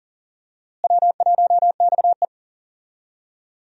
Morse code, heard as a keyed tone, about 40 wpm, 700 Hz. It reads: W1XE
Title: USA Call Signs - 4 Characters - Repeated 3x - 40wpm